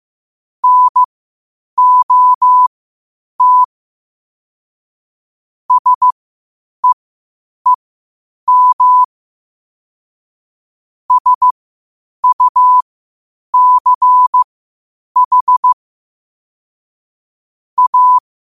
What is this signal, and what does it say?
Morse code, heard as a keyed tone, about 15 wpm, spaced as for 8 wpm, 1000 Hz. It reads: NOT SEEM SUCH A
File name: qotdfiles / 8wpm